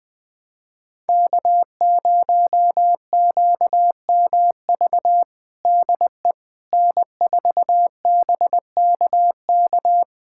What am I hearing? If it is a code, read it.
Morse code, heard as a keyed tone, about 20 wpm, 700 Hz. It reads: K0QMV DE N4BKK
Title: USA Callsign DE Callsign - 5 Characters - 20wpm